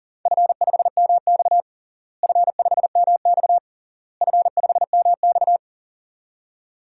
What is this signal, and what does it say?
Morse code, heard as a keyed tone, about 40 wpm, 700 Hz. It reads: F5MX F5MX F5MX
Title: DXCC-100 Call Signs - 4 Characters - Repeated 3x - 40wpm